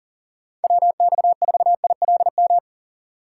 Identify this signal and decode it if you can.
Morse code, heard as a keyed tone, about 48 wpm, 700 Hz. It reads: WX4ILM
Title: Speed Racing - USA Call Signs 40wpm